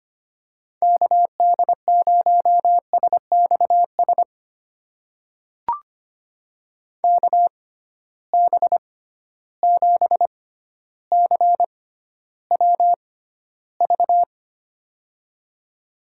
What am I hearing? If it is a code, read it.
Morse code, heard as a keyed tone, about 25 wpm, spaced as for 13 wpm, 700 Hz. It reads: KD0SXS KB7CWV
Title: USA Call Signs - 6 Characters - 6x Character Spacing - 25wpm